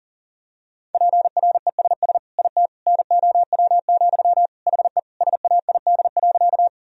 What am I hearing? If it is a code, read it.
Morse code, heard as a keyed tone, about 40 wpm, 700 Hz. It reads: PRESS IT NOW, HE SAID.
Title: Sentences from Top 1000 Words - Review Part 1 - 40wpm